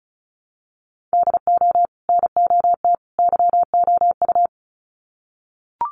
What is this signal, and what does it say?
Morse code, heard as a keyed tone, about 34 wpm, 700 Hz. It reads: DO NOT YOU
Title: Sets of 3 Words 35wpm